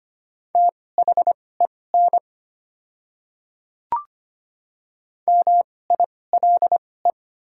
Morse code, heard as a keyed tone, about 25 wpm, spaced as for 18 wpm, 700 Hz. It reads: THEN MILE
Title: 2x Character Spacing - Top 500 Words - 25wpm